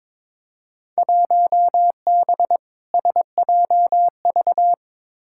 Morse code, heard as a keyed tone, about 22 wpm, 700 Hz. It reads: 1B SJV
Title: Field Day Exchanges Encore - 22wpm